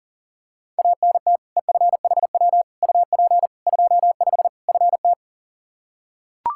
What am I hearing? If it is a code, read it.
Morse code, heard as a keyed tone, about 40 wpm, 700 Hz. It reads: ANT EFHW UP 25 FT
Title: K1USN SST QSO Practice 40wpm